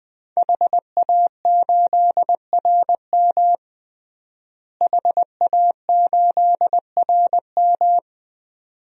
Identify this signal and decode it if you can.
Morse code, heard as a keyed tone, about 21 wpm, 700 Hz. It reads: HA8RM HA8RM
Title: DXCC-100 Call Signs - 5 Characters - Repeated 2x - 20wpm